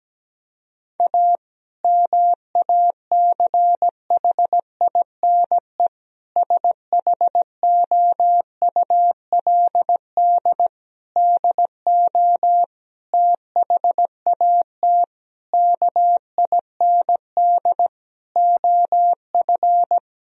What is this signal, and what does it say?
Morse code, heard as a keyed tone, about 17 wpm, 700 Hz. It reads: A MACHINE SHOULD DO THAT KIND OF
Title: Sentences from Top 1000 Words - Review Part 2 - 17wpm